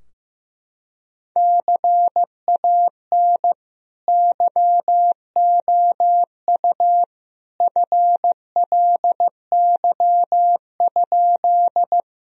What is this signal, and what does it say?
Morse code, heard as a keyed tone, about 15 wpm, 700 Hz. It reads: CAN YOU FLY?
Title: Sentences from Top 1000 Words - Review Part 2 - 15wpm